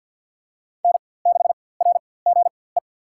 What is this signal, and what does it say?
Morse code, heard as a keyed tone, about 52 wpm, spaced as for 23 wpm, 700 Hz. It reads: N6RGE
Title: USA Call Signs - 5 Characters - 4x Character Spacing - 50wpm